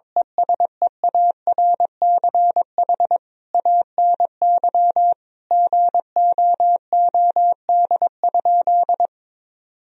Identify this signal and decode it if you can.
Morse code, heard as a keyed tone, about 21 wpm, 700 Hz. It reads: ESEARCH ANY GOOD?
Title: Sentences from Top 1000 Words - Review Part 4 - 22wpm